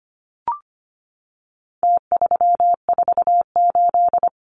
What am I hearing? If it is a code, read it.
Morse code, heard as a keyed tone, about 25 wpm, 700 Hz. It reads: T348
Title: Numbers - Contest Style - 4-Digit 25wpm